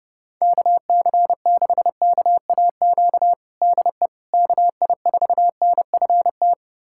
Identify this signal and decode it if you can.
Morse code, heard as a keyed tone, about 30 wpm, 700 Hz. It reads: KC6KAQ DE KI4NFT
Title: USA Callsign DE Callsign - 6 Characters - 30wpm